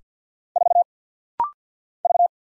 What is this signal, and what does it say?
Morse code, heard as a keyed tone, about 48 wpm, 700 Hz. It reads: 4 V
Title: Single Letter-Number - Rapid-Fire 50wpm